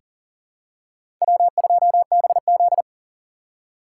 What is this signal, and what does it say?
Morse code, heard as a keyed tone, about 40 wpm, 700 Hz. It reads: W2BZ
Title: USA Call Signs - 4 Characters - 3x Character Spacing - 40wpm